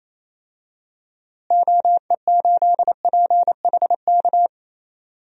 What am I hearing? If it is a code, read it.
Morse code, heard as a keyed tone, about 28 wpm, 700 Hz. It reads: OE8PHK
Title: DXCC-100 Call Signs - 6 Characters - Repeated 3x - 28wpm